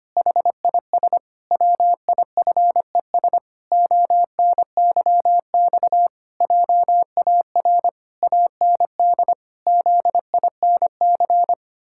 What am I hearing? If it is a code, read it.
Morse code, heard as a keyed tone, about 25 wpm, 700 Hz. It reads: HIS WIFES ONYX JAR AND ZINC
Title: Pangram Challenge v2 - 25wpm